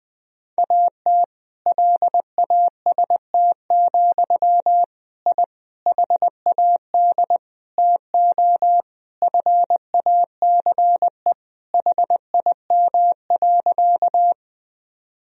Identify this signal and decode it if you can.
Morse code, heard as a keyed tone, about 20 wpm, 700 Hz. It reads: AT LAST, I HAD TO FACE HIM.
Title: Sentences from Top 1000 Words - Review Part 2 - 20wpm